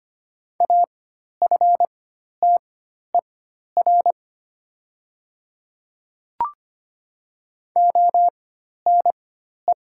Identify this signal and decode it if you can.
Morse code, heard as a keyed tone, about 25 wpm, spaced as for 12 wpm, 700 Hz. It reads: AFTER ONE
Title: 4x Character Spacing - Top 200 Words - 25wpm